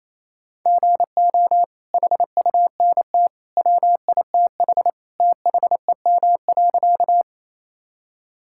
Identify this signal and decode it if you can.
Morse code, heard as a keyed tone, about 28 wpm, 700 Hz. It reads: GO HUNT WITH THEM.
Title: Sentences from Top 1600 Words 28wpm